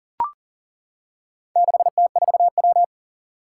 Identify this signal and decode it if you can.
Morse code, heard as a keyed tone, about 40 wpm, 700 Hz. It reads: 6T4W
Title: Four Challenging Letter-Number Combinations 40wpm